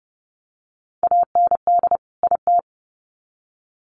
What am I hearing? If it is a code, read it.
Morse code, heard as a keyed tone, about 30 wpm, 700 Hz. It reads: AND IT
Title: Sets of 2 Words 30wpm